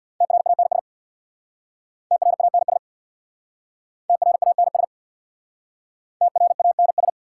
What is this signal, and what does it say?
Morse code, heard as a keyed tone, about 61 wpm, 690 Hz. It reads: TRANS TRANS TRANS TRANS
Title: Speed-Racing Prefixes and Suffixes 50wpm